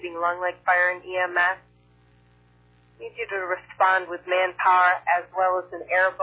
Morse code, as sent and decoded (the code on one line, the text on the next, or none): none